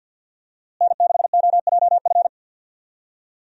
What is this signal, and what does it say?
Morse code, heard as a keyed tone, about 47 wpm, 700 Hz. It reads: N6OJF
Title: USA Call Signs - 5 Characters - Repeated 2x - 50wpm